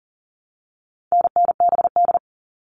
Morse code, heard as a keyed tone, about 40 wpm, 700 Hz. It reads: NNBD
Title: All Cut Numbers - 4-Digit 40wpm